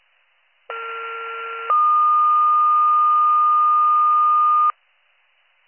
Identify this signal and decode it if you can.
Two-tone page, heard as a one-second tone, then a three-second tone, about 1370 then 1220 Hz. It is heard from 0.7 to 4.7 s.